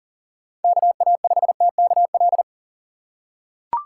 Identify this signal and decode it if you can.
Morse code, heard as a keyed tone, about 40 wpm, 700 Hz. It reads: KA5TKL
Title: USA Call Signs - 6 Characters - 3x Character Spacing - 40wpm